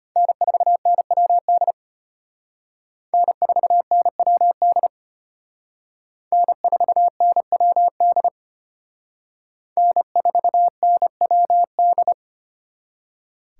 Morse code, heard as a keyed tone, about 32 wpm, 700 Hz. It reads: N4NWD N4NWD N4NWD N4NWD